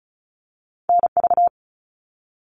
Morse code, heard as a keyed tone, about 36 wpm, 700 Hz. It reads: NV
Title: All Cut Numbers - 2-Digit 35wpm